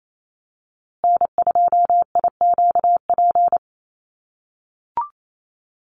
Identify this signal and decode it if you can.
Morse code, heard as a keyed tone, about 28 wpm, 700 Hz. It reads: N2IQP
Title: USA Call Signs Repeated 3x 28wpm